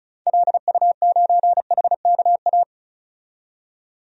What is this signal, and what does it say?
Morse code, heard as a keyed tone, about 34 wpm, 700 Hz. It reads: LU9HKA
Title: DXCC-100 Call Signs - 6 Characters - 35wpm